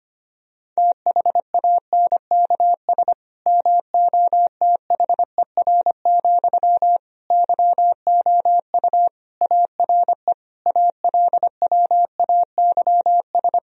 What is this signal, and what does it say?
Morse code, heard as a keyed tone, about 25 wpm, 700 Hz. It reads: THANKS MOTHER, YOU ARE ALWAYS
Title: Sentences from Top 1000 Words - Review Part 3 - 25wpm